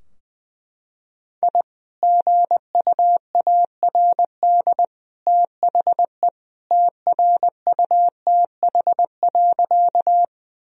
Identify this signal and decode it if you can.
Morse code, heard as a keyed tone, about 20 wpm, 700 Hz. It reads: I GUARD THE TRUTH.
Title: Sentences from Top 1000 Words - Review Part 4 - 20wpm